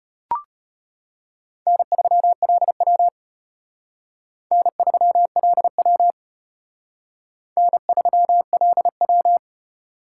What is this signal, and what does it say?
Morse code, heard as a keyed tone, about 32 wpm, 700 Hz. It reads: N3LW N3LW N3LW